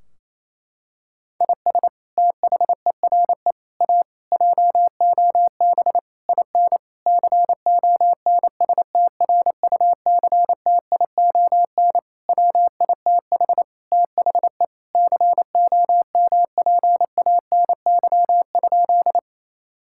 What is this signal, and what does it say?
Morse code, heard as a keyed tone, about 28 wpm, 700 Hz. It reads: IS THERE A JOB IN CONSTRUCTION WITH THE COMPANY?
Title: Sentences from Top 1600 Words 28wpm